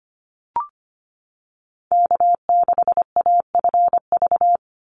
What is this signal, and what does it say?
Morse code, heard as a keyed tone, about 25 wpm, 700 Hz. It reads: K6AFV
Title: USA Call Signs 25wpm